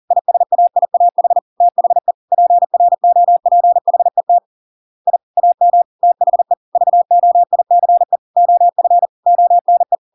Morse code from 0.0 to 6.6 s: ISAIAH THE PROPHET I AM THE